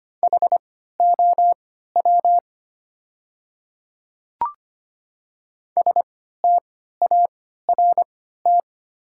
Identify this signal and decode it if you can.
Morse code, heard as a keyed tone, about 25 wpm, spaced as for 14 wpm, 690 Hz. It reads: HOW START
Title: 3x Character Spacing - Top 300 Words - 25wpm